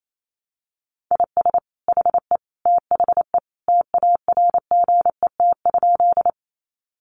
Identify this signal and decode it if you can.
Morse code, heard as a keyed tone, about 28 wpm, 700 Hz. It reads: IS HE THE TARGET?
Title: Sentences from Top 1200 Words 28wpm